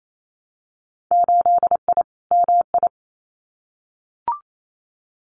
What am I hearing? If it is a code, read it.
Morse code, heard as a keyed tone, about 28 wpm, 700 Hz. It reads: 8I MI
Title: Winter Field Day Exchanges Encore 28wpm